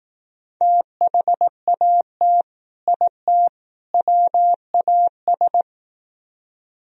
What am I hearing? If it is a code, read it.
Morse code, heard as a keyed tone, about 19 wpm, 700 Hz. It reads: THAT IT WAS
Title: Speed-Racing - Sets of 3 Words 15wpm